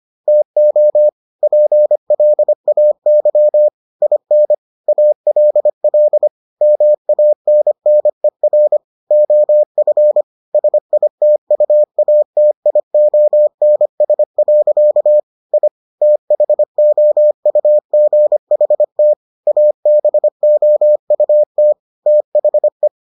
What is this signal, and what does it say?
Morse code, heard as a keyed tone, about 25 wpm, 600 Hz. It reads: TO PLAY IN ALL MANNER OF SITUATIONS. I THOUGHT ABOUT THE